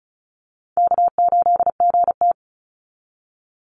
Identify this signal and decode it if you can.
Morse code, heard as a keyed tone, about 34 wpm, 700 Hz. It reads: K8GT
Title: Call Signs Repeated 2x - 35wpm